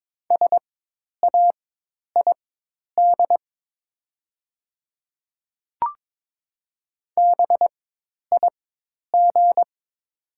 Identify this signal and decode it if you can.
Morse code, heard as a keyed tone, about 22 wpm, spaced as for 10 wpm, 700 Hz. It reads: SAID BIG
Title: 4x Character Spacing - Top 200 Words - 22wpm